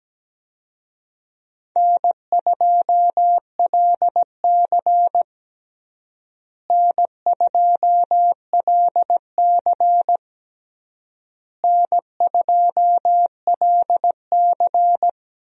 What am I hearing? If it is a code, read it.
Morse code, heard as a keyed tone, about 17 wpm, 700 Hz. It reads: N2LC N2LC N2LC
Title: USA Call Signs - 4 Characters - Repeated 3x - 17wpm